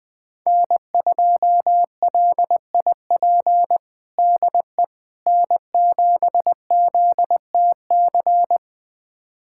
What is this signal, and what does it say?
Morse code, heard as a keyed tone, about 20 wpm, 700 Hz. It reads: N2LIP DE N7ZTC
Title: USA Callsign DE Callsign - 5 Characters - 20wpm